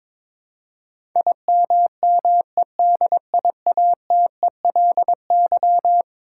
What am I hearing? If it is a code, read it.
Morse code, heard as a keyed tone, about 22 wpm, 700 Hz. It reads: IMMEDIATELY
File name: practice / 22wpm / Eleven Letter Words - 22wpm